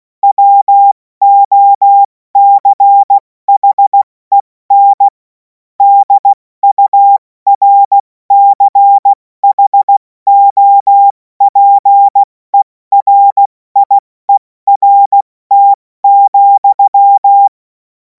Morse code, heard as a keyed tone, about 16 wpm, 800 Hz. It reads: WOCHEN DURCHOPERIERT,
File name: Morsekurs / Witze / Witz 11. August 2026 70 F80